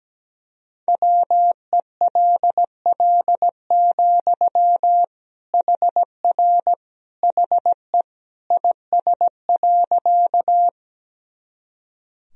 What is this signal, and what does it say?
Morse code, heard as a keyed tone, about 17 wpm, 700 Hz. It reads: WELL, HR HE IS.
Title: Sentences from Top 200 Words and Abbreviations 17wpm